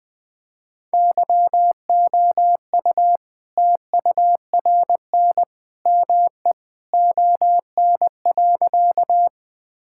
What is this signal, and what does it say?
Morse code, heard as a keyed tone, about 21 wpm, 700 Hz. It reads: YOU TURN ME ON.
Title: All-Age Appropriate Jokes - 20wpm